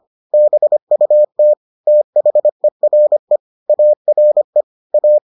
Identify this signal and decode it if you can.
Morse code, heard as a keyed tone, about 25 wpm, 600 Hz. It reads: BUT THERE ARE A